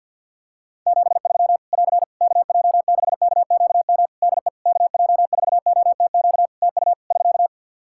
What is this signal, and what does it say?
Morse code, heard as a keyed tone, about 51 wpm, 700 Hz. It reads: 73 <AR> KJ6KQM DE KJ4OTQ TU <SK>
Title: Practice QSO - Exchange 4 - Ending - 50wpm